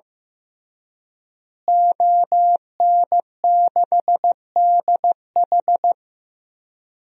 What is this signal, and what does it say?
Morse code, heard as a keyed tone, about 15 wpm, 700 Hz. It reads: ON6DH
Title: DXCC-100 Call Signs - 5 Characters - Repeated 3x - 15wpm